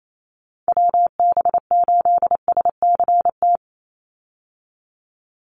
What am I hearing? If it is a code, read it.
Morse code, heard as a keyed tone, about 28 wpm, 700 Hz. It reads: WB8SCT
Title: Call Signs - 28wpm